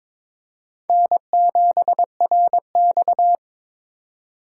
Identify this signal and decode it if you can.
Morse code, heard as a keyed tone, about 22 wpm, 700 Hz. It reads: N7RX
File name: practice / 22wpm / Four Letter-Number Combinations - 22wpm